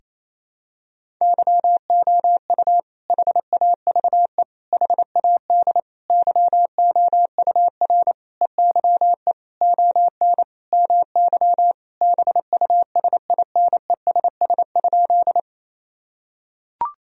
Morse code, heard as a keyed tone, about 27 wpm, 700 Hz. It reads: YOU HAVE HAD YOUR EYE ON MY BUSINESS?
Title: Sentences from Top 1000 Words - Review Part 1 - 28wpm